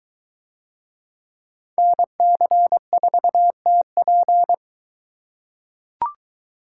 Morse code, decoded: NC4TP

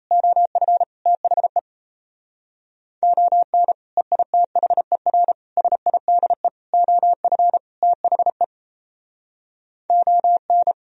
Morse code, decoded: OF THE ON EITHER SIDE OF THE ON

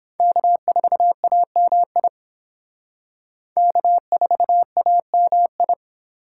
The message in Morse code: K4AMI K4AMI

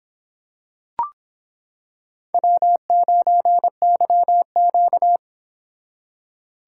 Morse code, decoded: W9YQ

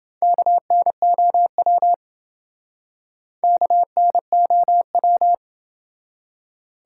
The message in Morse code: KNOW KNOW

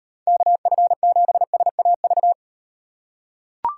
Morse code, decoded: KF7SAV

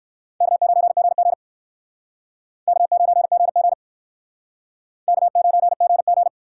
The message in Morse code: K9GG K9GG K9GG

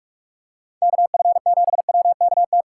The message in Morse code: KF7WKT

